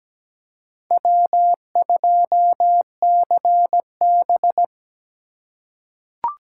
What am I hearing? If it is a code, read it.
Morse code, heard as a keyed tone, about 17 wpm, 700 Hz. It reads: W2CB
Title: USA Call Signs - 4 Characters - 4x Character Spacing - 17wpm